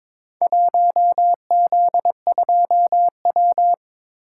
Morse code, heard as a keyed tone, about 22 wpm, 700 Hz. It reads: 1Z2W
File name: practice / 22wpm / Four Letter-Number Combinations - 22wpm